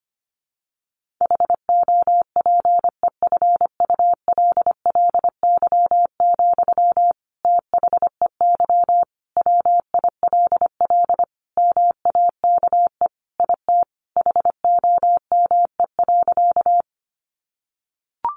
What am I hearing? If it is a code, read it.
Morse code, heard as a keyed tone, about 25 wpm, 700 Hz. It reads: HOPEFULLY, THEY WILL MAKE IT HOME.
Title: Sentences from Top 1500 Words 25wpm